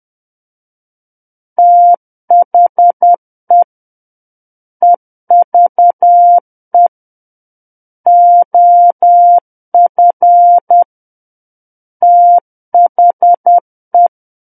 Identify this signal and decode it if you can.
Morse code, heard as a keyed tone, about 10 wpm, 700 Hz. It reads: THE EVE OF THE
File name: Book One / 01-Chapter 1 - 10 WPM